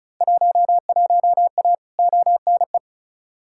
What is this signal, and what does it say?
Morse code, heard as a keyed tone, about 36 wpm, 700 Hz. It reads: 11A ONE
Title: Field Day Exchanges Encore - 35wpm